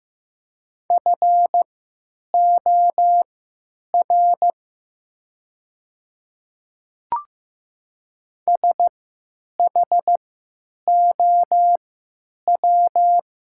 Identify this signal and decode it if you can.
Morse code, heard as a keyed tone, about 15 wpm, spaced as for 9 wpm, 700 Hz. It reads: FOR SHOW
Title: 3x Character Spacing - Top 300 Words - 15wpm